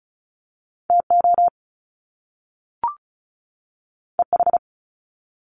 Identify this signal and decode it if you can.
Morse code, heard as a keyed tone, about 36 wpm, 700 Hz. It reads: TO EH
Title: Top Two Letter Combinations in Words 35wpm